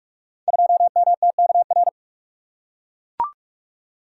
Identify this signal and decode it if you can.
Morse code, heard as a keyed tone, about 44 wpm, 700 Hz. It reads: 2MTKR
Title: Code Groups - 5 Letter:Numbers - 45wpm